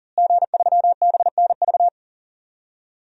Morse code, heard as a keyed tone, about 40 wpm, 700 Hz. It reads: G3BNV
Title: DXCC-100 Call Signs - 5 Characters - Repeated 3x - 40wpm